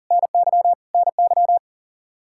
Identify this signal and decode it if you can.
Morse code, heard as a keyed tone, about 40 wpm, 700 Hz. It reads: NY NY
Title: Practice QSO - Exchange 1 - Beginning - 40wpm